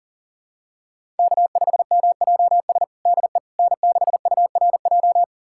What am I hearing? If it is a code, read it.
Morse code, heard as a keyed tone, about 37 wpm, 700 Hz. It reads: K5MJS DE N6URJ
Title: USA Callsign DE Callsign - 5 Characters - 40wpm